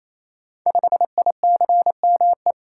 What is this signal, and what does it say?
Morse code, heard as a keyed tone, about 27 wpm, 700 Hz. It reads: 5ICME
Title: Code Groups - 5 Letter:Numbers - 28wpm